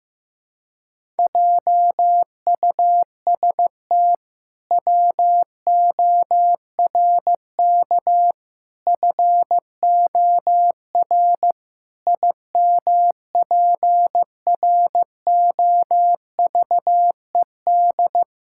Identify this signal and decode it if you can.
Morse code, heard as a keyed tone, about 15 wpm, 700 Hz. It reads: JUST WORK FOR IMPROVED
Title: Pangram Challenge v2 - 15wpm